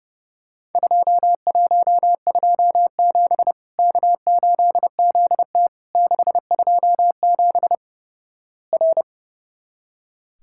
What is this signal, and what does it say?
Morse code, heard as a keyed tone, about 29 wpm, 700 Hz. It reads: 2127 K8ZT 627 R
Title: Worked All Europe DX Contest - QTC Exchange - 30wpm